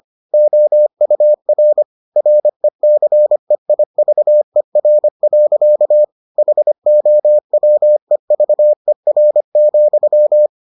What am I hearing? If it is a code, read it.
Morse code, heard as a keyed tone, about 25 wpm, 600 Hz. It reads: OUR RECEIVER. HOWEVER,